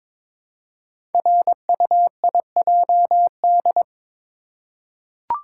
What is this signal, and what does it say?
Morse code, heard as a keyed tone, about 22 wpm, 700 Hz. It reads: RUIJD
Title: Code Groups - 5 Letters - 22wpm